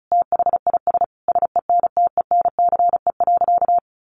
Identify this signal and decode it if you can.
Morse code, heard as a keyed tone, about 34 wpm, 700 Hz. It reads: THIS SENTENCE.
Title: Additional Easy Phrases 35wpm